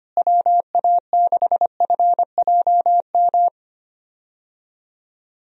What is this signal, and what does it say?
Morse code, heard as a keyed tone, about 25 wpm, 700 Hz. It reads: WA6FJM
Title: USA Call Signs - 6 Characters - Repeated 2x - 25wpm